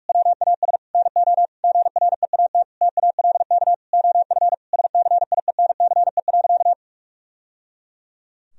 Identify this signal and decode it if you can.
Morse code, heard as a keyed tone, about 46 wpm, 700 Hz. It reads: WAS NO GREAT TALK OF SCIENCE.
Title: Sentences from Top 1000 Words - Review Part 2 - 45wpm